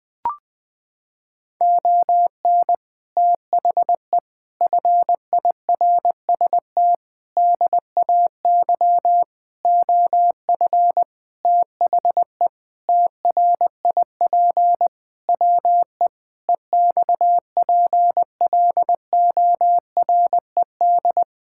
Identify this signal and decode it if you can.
Morse code, heard as a keyed tone, about 21 wpm, 700 Hz. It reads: ON THE FIRST DAY OF THE TRIP WE EXPLORED
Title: Sentences with 6-Word Phrases - 20wpm